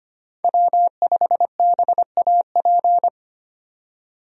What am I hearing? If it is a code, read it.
Morse code, heard as a keyed tone, about 25 wpm, 700 Hz. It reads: W5BAP
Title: USA Call Signs - 5 Characters - 25wpm